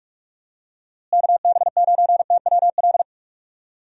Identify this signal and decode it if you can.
Morse code, heard as a keyed tone, about 43 wpm, 700 Hz. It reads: KB9TWL
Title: USA Call Signs - 6 Characters - Repeated 2x - 45wpm